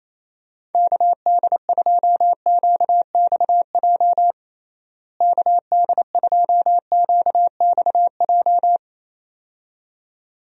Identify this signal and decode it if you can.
Morse code, heard as a keyed tone, about 28 wpm, 700 Hz. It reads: KD2QXJ KD2QXJ
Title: USA Call Signs - 6 Characters - Repeated 2x - 28wpm